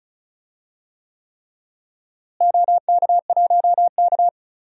Morse code, heard as a keyed tone, about 36 wpm, 700 Hz. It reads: OK1K
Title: DXCC-100 Call Signs - 4 Characters - Repeated 3x - 35wpm